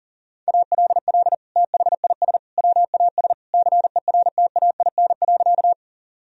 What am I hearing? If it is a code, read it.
Morse code, heard as a keyed tone, about 37 wpm, 700 Hz. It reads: ALL THIS WAS CERTAIN.
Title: Sentences from Top 1000 Words - Review Part 2 - 40wpm